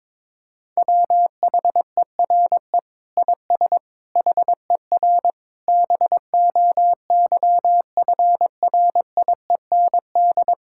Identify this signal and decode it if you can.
Morse code, heard as a keyed tone, about 22 wpm, 700 Hz. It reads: WHERE IS HER BOYFRIEND
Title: Sentences from Top 1000 Words - Review Part 5 - 22wpm